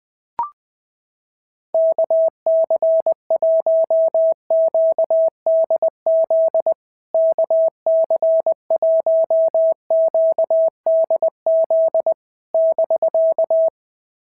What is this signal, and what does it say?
Morse code, heard as a keyed tone, about 20 wpm, 660 Hz. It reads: KC1QDZ KC1QDZ <BK>
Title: POTA QSO Practice - 20wpm